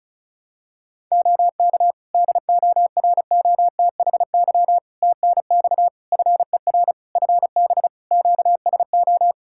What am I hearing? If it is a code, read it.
Morse code, heard as a keyed tone, about 34 wpm, 700 Hz. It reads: OK DOROTHY TNX FER FB QSO
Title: Practice QSO - Exchange 4 - Ending - 35wpm